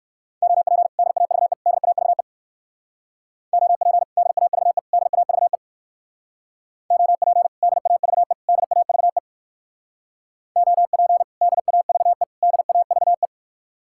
Morse code, heard as a keyed tone, about 61 wpm, 700 Hz. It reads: OP DAVE DAVE OP DAVE DAVE OP DAVE DAVE OP DAVE DAVE